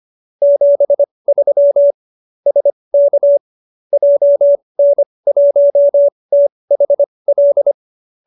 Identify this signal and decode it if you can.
Morse code, heard as a keyed tone, about 25 wpm, spaced as for 20 wpm, 570 Hz. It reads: 73 SK JN1THL